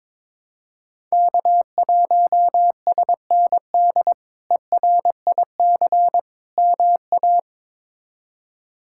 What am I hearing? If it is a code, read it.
Morse code, heard as a keyed tone, about 22 wpm, 700 Hz. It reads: K1SND ERIC MA
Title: K1USN SST QSO Practice - 22wpm